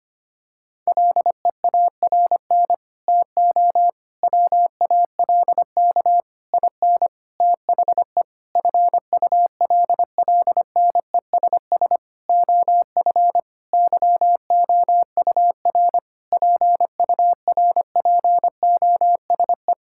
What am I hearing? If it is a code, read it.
Morse code, heard as a keyed tone, about 25 wpm, 700 Hz. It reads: LEARN TO WALK IN THE FULLNESS OF YOUR PURPOSE
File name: Sets of 9 Words - 25wpm